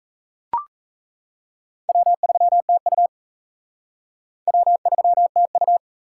Morse code, heard as a keyed tone, about 33 wpm, 700 Hz. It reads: W3TU W3TU
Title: Speed Racing - USA 1x2 Call Signs 28wpm